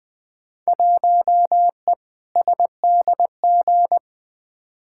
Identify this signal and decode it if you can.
Morse code, heard as a keyed tone, about 20 wpm, 700 Hz. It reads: 1E SDG
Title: Field Day Exchanges - 20wpm